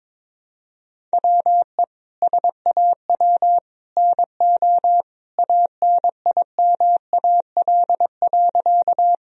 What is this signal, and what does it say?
Morse code, heard as a keyed tone, about 21 wpm, 700 Hz. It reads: WE SAW NO ANIMAL.
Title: Sentences from Top 1000 Words - Review Part 1 - 22wpm